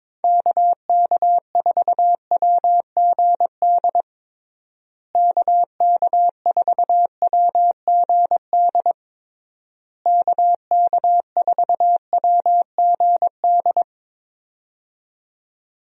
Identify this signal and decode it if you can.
Morse code, heard as a keyed tone, about 22 wpm, 700 Hz. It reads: KK4WGD KK4WGD KK4WGD
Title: USA Call Signs - 6 Characters - Repeated 3x - 22wpm